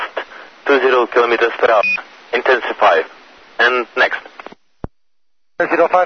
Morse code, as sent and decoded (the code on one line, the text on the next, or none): none